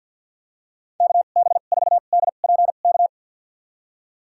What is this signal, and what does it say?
Morse code, heard as a keyed tone, about 52 wpm, spaced as for 37 wpm, 700 Hz. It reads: KB4DPK
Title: USA Call Signs - 2x Character Spacing - 50wpm